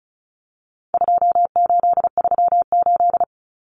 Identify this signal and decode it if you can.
Morse code, heard as a keyed tone, about 34 wpm, 700 Hz. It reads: 2838
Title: Common Cut Numbers - 4-Digit 35wpm